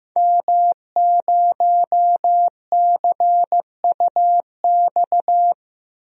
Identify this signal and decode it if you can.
Morse code, heard as a keyed tone, about 15 wpm, 700 Hz. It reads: M0CUX
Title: DXCC-100 Call Signs - 5 Characters - Repeated 3x - 15wpm